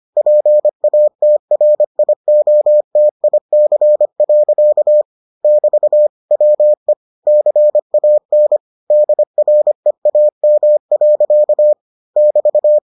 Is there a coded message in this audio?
Yes: Morse code from 0.2 to 12.8 s, heard as a keyed tone, about 25 wpm, 600 Hz. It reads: PATRIOTIC. <BT> WE CAN DREAM. <BT>